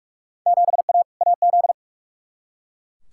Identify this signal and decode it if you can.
Morse code, heard as a keyed tone, about 41 wpm, 700 Hz. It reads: 7A AZ
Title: Field Day Exchanges - 45wpm